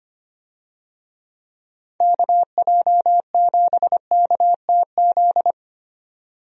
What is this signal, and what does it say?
Morse code, heard as a keyed tone, about 25 wpm, 700 Hz. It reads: KJ7KTZ
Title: USA Call Signs - 6 Characters - 25wpm